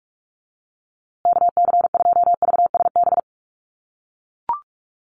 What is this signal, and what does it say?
Morse code, heard as a keyed tone, about 44 wpm, 700 Hz. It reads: KC2VSB
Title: USA Call Signs Repeated 3x 45wpm